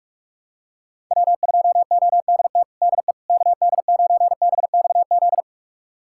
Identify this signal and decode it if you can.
Morse code, heard as a keyed tone, about 45 wpm, 700 Hz. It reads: W2ODT DE KD9BXZ
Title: USA Callsign DE Callsign 45wpm